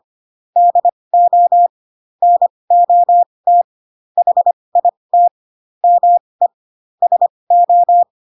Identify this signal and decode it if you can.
Morse code, heard as a keyed tone, about 25 wpm, spaced as for 20 wpm, 700 Hz. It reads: DO NOT HIT ME SO
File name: Pinocchio / Pinocchio ch1 15 25 ik1hge